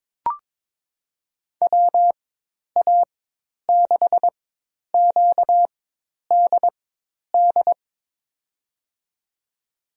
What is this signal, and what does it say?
Morse code, heard as a keyed tone, about 23 wpm, spaced as for 10 wpm, 700 Hz. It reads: WA6QDD
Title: USA Call Signs - 6 Characters - 4x Character Spacing - 22wpm